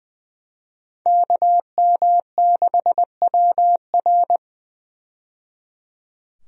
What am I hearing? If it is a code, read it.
Morse code, heard as a keyed tone, about 20 wpm, 700 Hz. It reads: KM6WR
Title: USA Call Signs - 5 Characters - 20wpm